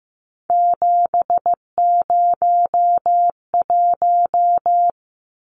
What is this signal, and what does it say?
Morse code, heard as a keyed tone, about 15 wpm, 700 Hz. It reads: 701
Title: Numbers - 3-Digit 15wpm